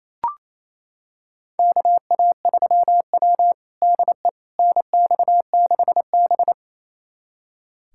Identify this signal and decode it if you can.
Morse code, heard as a keyed tone, about 28 wpm, 690 Hz. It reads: KA3W DE NX6B
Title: USA Callsign DE Callsign - 4 Characters - 28wpm